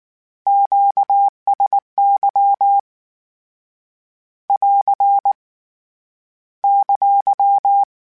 Morse code, heard as a keyed tone, about 19 wpm, 800 Hz. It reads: QSY <AR> !
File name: Q Calls / Q Calls-mp3s / Q Calls Once @18